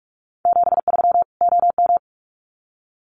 Morse code, heard as a keyed tone, about 44 wpm, 700 Hz. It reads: 73 OM
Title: Common QSO Phrases 45wpm